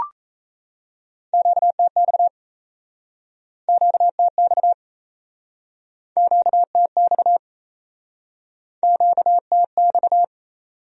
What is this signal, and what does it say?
Morse code, heard as a keyed tone, about 37 wpm, 700 Hz. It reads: QTX QTX QTX QTX